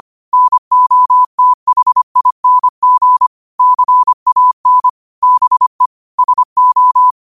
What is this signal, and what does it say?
Morse code, heard as a keyed tone, about 25 wpm, 1000 Hz. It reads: NOTHING CAN BE SO
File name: qotdfiles / 25wpm